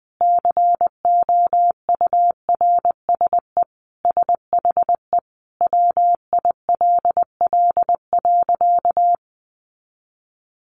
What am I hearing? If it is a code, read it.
Morse code, heard as a keyed tone, about 20 wpm, 700 Hz. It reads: COURSE SHE WILL.
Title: Sentences from Top 500 Words 20wpm